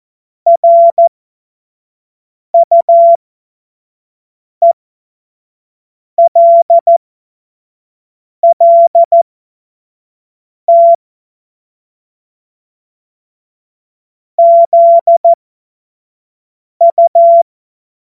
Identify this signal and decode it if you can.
Morse code, heard as a keyed tone, about 14 wpm, spaced as for 5 wpm, 680 Hz. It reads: RUELLT ZU
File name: Morsekurs / Witze / Witz 25. Juli 2023 25 F70